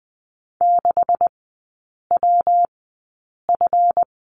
Morse code, heard as a keyed tone, about 20 wpm, spaced as for 8 wpm, 690 Hz. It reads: 6WF
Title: 159 Lesson 26e - Letter g - Three Characters 20wpm